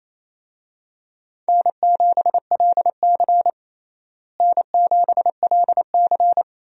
Morse code, heard as a keyed tone, about 28 wpm, 700 Hz. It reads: N7LC N7LC
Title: USA Call Signs - 4 Characters - Repeated 2x - 28wpm